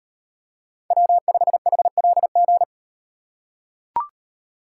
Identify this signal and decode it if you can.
Morse code, heard as a keyed tone, about 35 wpm, 700 Hz. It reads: W5HLG